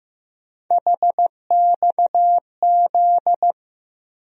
Morse code, heard as a keyed tone, about 15 wpm, 700 Hz. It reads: HXZ
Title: USA Call Sign Postfixes - 3 Letters 15wpm